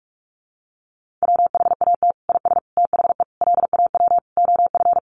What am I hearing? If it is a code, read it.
Morse code, heard as a keyed tone, about 44 wpm, 700 Hz. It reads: WHAT IS THE LAW OF
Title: Sentences from Top 1300 Words 45wpm